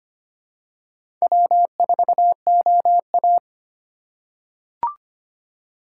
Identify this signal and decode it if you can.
Morse code, heard as a keyed tone, about 25 wpm, 690 Hz. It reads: W4OA
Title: USA Call Signs - 4x Character Spacing - 25wpm